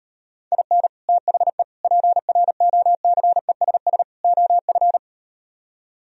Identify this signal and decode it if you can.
Morse code, heard as a keyed tone, about 39 wpm, 700 Hz. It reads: IN THE PROCESS OF